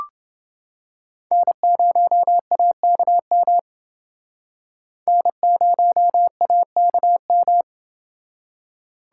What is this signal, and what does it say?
Morse code, heard as a keyed tone, about 30 wpm, 700 Hz. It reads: N0AKM N0AKM